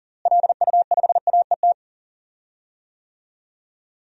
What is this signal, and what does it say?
Morse code, heard as a keyed tone, about 39 wpm, 700 Hz. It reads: LU5AET
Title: DXCC-100 Call Signs - 6 Characters - Repeated 2x - 40wpm